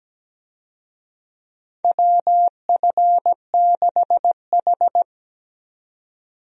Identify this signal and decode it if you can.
Morse code, heard as a keyed tone, about 17 wpm, 700 Hz. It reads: WF6H
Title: USA Call Signs - 4 Characters - 17wpm